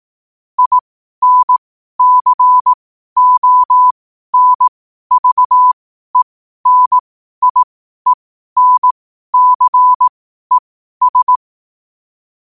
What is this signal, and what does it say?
Morse code, heard as a keyed tone, about 18 wpm, spaced as for 12 wpm, 1000 Hz. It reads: INCONVENIENCES
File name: qotdfiles / 12wpm